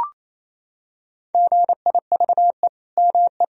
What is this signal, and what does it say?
Morse code, heard as a keyed tone, about 28 wpm, 700 Hz. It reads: GIVE ME
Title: Sentences from Top 1000 Words - Review Part 3 - 28wpm